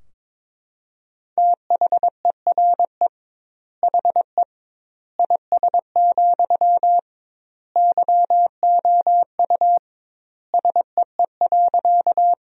Morse code, heard as a keyed tone, about 22 wpm, 700 Hz. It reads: THERE HE IS, YOU SEE.
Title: 2x Word-Spacing - Sentences from Top 100 Words - 22wpm